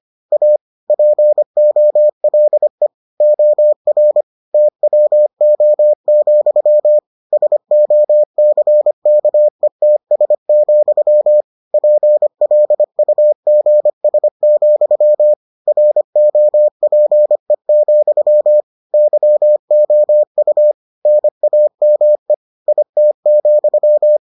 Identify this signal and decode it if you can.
Morse code, heard as a keyed tone, about 25 wpm, 600 Hz. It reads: A POLE OR TWO, SOCKETS, PLUGS, ROPE, YOU NAME IT,